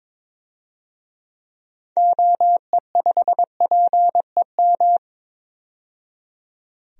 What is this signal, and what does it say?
Morse code, heard as a keyed tone, about 21 wpm, 700 Hz. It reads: OE5PEM
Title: DXCC-100 Call Signs - 6 Characters - 22wpm